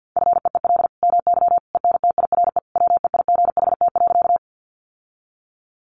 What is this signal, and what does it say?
Morse code, heard as a keyed tone, about 50 wpm, 700 Hz. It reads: FEEL MY ENTIRE WEIGHT.
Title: Sentences from Top 800 Words 50wpm